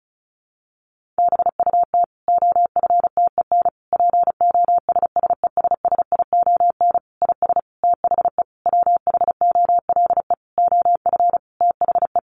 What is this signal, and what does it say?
Morse code, heard as a keyed tone, about 34 wpm, 700 Hz. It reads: BUT OFTEN POSSESSION IS THE WHOLE OF THE
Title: Sentences from Top 1400 Words 35wpm